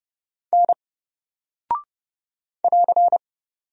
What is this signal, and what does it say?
Morse code, heard as a keyed tone, about 30 wpm, 700 Hz. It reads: N <AR>
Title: Single Character (Letter:Number:Punctuation:Prosign) Alt 30wpm